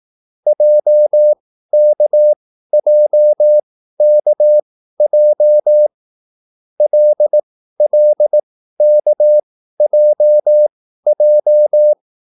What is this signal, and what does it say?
Morse code, heard as a keyed tone, about 18 wpm, spaced as for 13 wpm, 600 Hz. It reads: JKJKJ LLKJJ